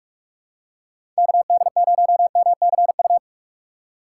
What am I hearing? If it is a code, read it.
Morse code, heard as a keyed tone, about 43 wpm, 700 Hz. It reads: KD0MCU
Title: USA Call Signs - 6 Characters - 3x Character Spacing - 45wpm